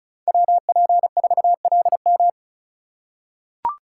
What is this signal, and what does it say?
Morse code, heard as a keyed tone, about 36 wpm, 700 Hz. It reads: WP4LM
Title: USA Call Signs - 5 Characters - 3x Character Spacing - 35wpm